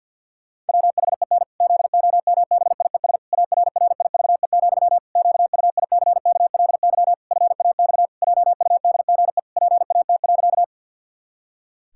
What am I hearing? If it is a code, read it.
Morse code, heard as a keyed tone, about 51 wpm, 700 Hz. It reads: WHEN ZOMBIES ARRIVE, QUICKLY FAX JUDGE PAT.
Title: Pangram Challenge v2 - 50wpm